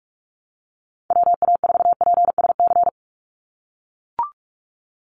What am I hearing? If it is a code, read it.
Morse code, heard as a keyed tone, about 44 wpm, 700 Hz. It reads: WA4PSC
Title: Call Signs - 45wpm